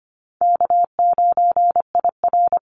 Morse code, heard as a keyed tone, about 25 wpm, 700 Hz. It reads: K9IR
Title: Call Signs - 25wpm